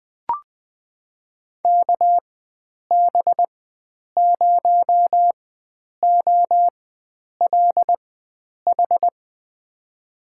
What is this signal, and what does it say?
Morse code, heard as a keyed tone, about 21 wpm, spaced as for 9 wpm, 700 Hz. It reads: KB0OLH
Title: USA Call Signs - 4x Character Spacing - 20wpm